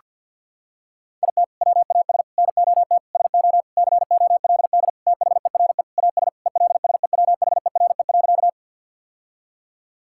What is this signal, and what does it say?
Morse code, heard as a keyed tone, about 50 wpm, 700 Hz. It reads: IT WAS NOT SO COLD THERE AS ELSEWHERE.
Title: Sentences from Top 1600 Words 50wpm